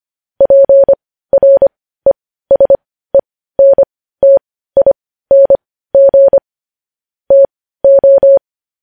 Morse code, heard as a keyed tone, about 25 wpm, spaced as for 15 wpm, 550 Hz. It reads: PRESENTING TO
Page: news headlines in morse code at 15 wpm